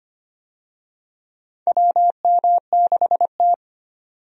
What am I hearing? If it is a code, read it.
Morse code, heard as a keyed tone, about 25 wpm, 700 Hz. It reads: WM6T
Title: USA Call Signs - 4 Characters - 25wpm